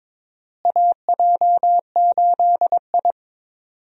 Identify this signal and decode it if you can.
Morse code, heard as a keyed tone, about 22 wpm, 700 Hz. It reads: AJ8I
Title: USA Call Signs - 4 Characters - Repeated 2x - 22wpm